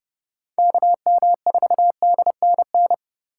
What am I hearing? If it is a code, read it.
Morse code, heard as a keyed tone, about 29 wpm, 700 Hz. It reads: KM4DNN